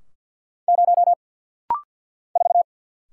Morse code, heard as a keyed tone, about 50 wpm, 700 Hz. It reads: 0 4
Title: Single Letter-Number Alt - Rapid-Fire 50wpm